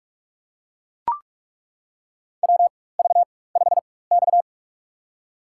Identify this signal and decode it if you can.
Morse code, heard as a keyed tone, about 42 wpm, spaced as for 21 wpm, 700 Hz. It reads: WV5X
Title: USA Call Signs - 4 Characters - 4x Character Spacing - 45wpm